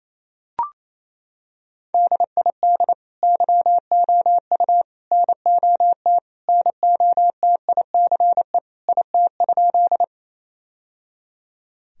Morse code, heard as a keyed tone, about 27 wpm, 690 Hz. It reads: DID YOU NOT NOTICE IT?
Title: Sentences from Top 1000 Words - Review Part 2 - 28wpm